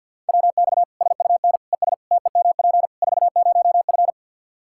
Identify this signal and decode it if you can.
Morse code, heard as a keyed tone, about 50 wpm, 700 Hz. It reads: WX SUN ES TEMP 40F
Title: Practice QSO - Exchange 2 - Info - 50wpm